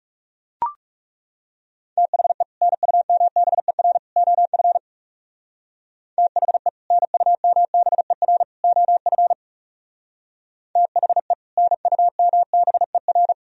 Morse code, heard as a keyed tone, about 37 wpm, 700 Hz. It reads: THE NUMBER OF THE NUMBER OF THE NUMBER
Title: Speed-Racing - Sets of 3 Words 30wpm